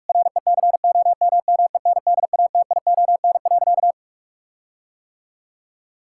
Morse code, heard as a keyed tone, about 44 wpm, 700 Hz. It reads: RECOMMENDATION.
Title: Sentences from Top 1600 Words 45wpm